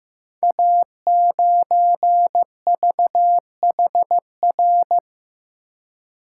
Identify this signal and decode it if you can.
Morse code, heard as a keyed tone, about 15 wpm, 700 Hz. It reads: A9VHR
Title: Code Groups - 5 Letter:Numbers - 15wpm